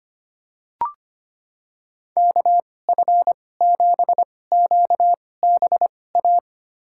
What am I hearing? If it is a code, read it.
Morse code, heard as a keyed tone, about 25 wpm, spaced as for 18 wpm, 700 Hz. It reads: KF7QBA
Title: USA Call Signs - 2x Character Spacing - 25wpm